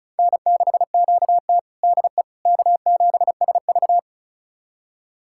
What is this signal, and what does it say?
Morse code, heard as a keyed tone, about 34 wpm, 700 Hz. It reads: N6QT DE K7SV
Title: USA Callsign DE Callsign - 4 Characters - 35wpm